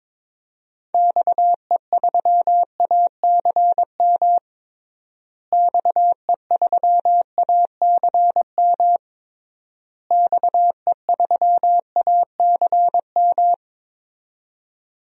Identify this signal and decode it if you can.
Morse code, heard as a keyed tone, about 22 wpm, 700 Hz. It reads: XE3ACM XE3ACM XE3ACM
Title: DXCC-100 Call Signs - 6 Characters - Repeated 3x - 22wpm